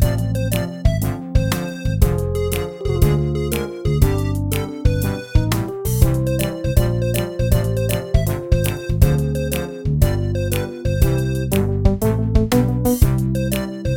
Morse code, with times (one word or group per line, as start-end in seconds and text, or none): none